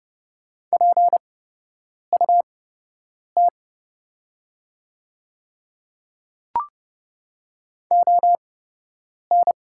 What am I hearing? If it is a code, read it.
Morse code, heard as a keyed tone, about 30 wpm, spaced as for 8 wpm, 700 Hz. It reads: PUT ON
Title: 8x Character Spacing - Top 200 Words - 30wpm